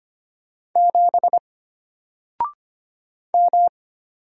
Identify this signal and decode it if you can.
Morse code, heard as a keyed tone, about 25 wpm, 700 Hz. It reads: 7 M
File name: Single-Letter-Number-25wpm